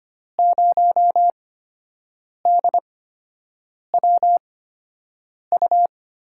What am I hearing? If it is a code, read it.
Morse code, heard as a keyed tone, about 25 wpm, spaced as for 7 wpm, 700 Hz. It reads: 0DWU
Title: USA Call Signs - 6 Characters - 8x Character Spacing - 25wpm